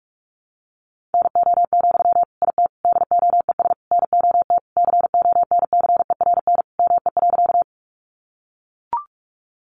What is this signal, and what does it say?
Morse code, heard as a keyed tone, about 44 wpm, 700 Hz. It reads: NO, IT DOES NOT CONCERN ME.
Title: Sentences from Top 1000 Words 45wpm